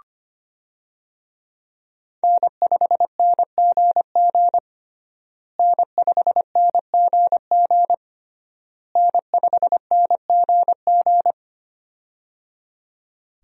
Morse code, heard as a keyed tone, about 25 wpm, 700 Hz. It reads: N5NGG N5NGG N5NGG
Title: USA Call Signs - 5 Characters - Repeated 3x - 25wpm